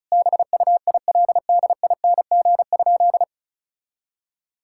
Morse code, heard as a keyed tone, about 34 wpm, 700 Hz. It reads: BUILDING?
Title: Sentences from Top 1000 Words - Review Part 5 - 35wpm